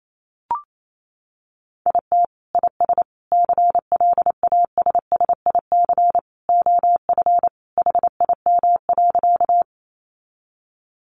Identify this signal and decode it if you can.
Morse code, heard as a keyed tone, about 28 wpm, 700 Hz. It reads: IT IS CLASSIC OF HIM.
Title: Sentences from Top 1000 Words 28wpm